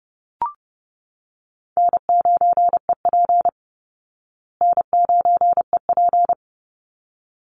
Morse code, heard as a keyed tone, about 30 wpm, 700 Hz. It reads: N9EP N9EP
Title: Easy Call Signs Repeated 2x - 30wpm